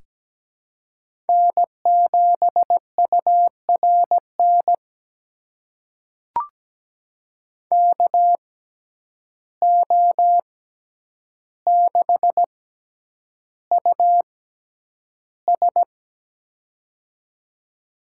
Morse code, heard as a keyed tone, about 17 wpm, spaced as for 9 wpm, 700 Hz. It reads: N7URN KO6US
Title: USA Call Signs - 6x Character Spacing - 17wpm